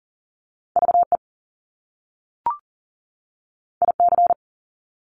Morse code, heard as a keyed tone, about 41 wpm, 700 Hz. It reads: VE IC
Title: Top Two Letter Combinations in Words 40wpm